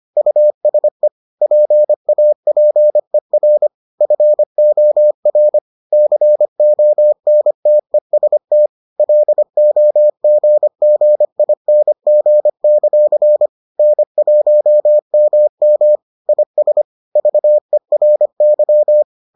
Morse code, heard as a keyed tone, about 25 wpm, 600 Hz. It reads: USE PAPER FOR CONTEST LOGGING; N1MM IS VERY